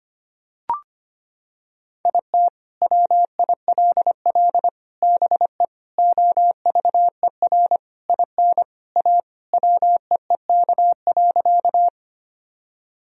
Morse code, heard as a keyed tone, about 25 wpm, 700 Hz. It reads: IT WILL BE OVER IN A WEEK.
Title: Sentences from Top 1000 Words - Review Part 2 - 25wpm